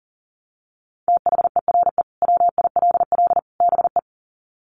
Morse code, heard as a keyed tone, about 40 wpm, 700 Hz. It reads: THERE WILL BE
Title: Sets of 3 Words 40wpm